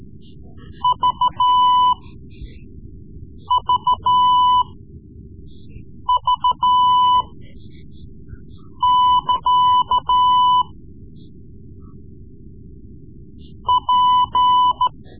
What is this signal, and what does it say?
Morse code, heard as a keyed tone, about 14 wpm, spaced as for 5 wpm, 1000 Hz. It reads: VVV<KA> P
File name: Morse Lessons / Morse Lessons / Tape-3B